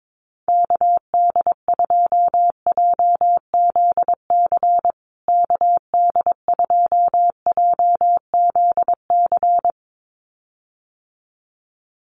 Morse code, heard as a keyed tone, about 22 wpm, 700 Hz. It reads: KD2JZC KD2JZC
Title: USA Call Signs Repeated 2x 22wpm